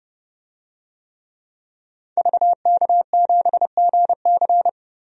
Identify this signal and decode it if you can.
Morse code, heard as a keyed tone, about 32 wpm, 700 Hz. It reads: VK7GC
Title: DXCC-100 Call Signs - 5 Characters - Repeated 3x - 30wpm